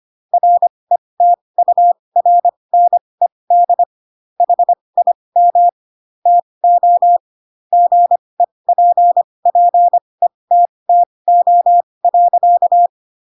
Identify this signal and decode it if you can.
Morse code, heard as a keyed tone, about 25 wpm, spaced as for 20 wpm, 700 Hz. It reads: RETURNED HIM TO GEPPETTO.